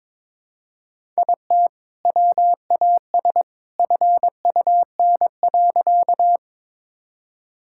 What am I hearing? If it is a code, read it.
Morse code, heard as a keyed tone, about 22 wpm, 700 Hz. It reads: IT WAS FUN.
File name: practice / 22wpm / Sentences with 2-Word Phrases - 22wpm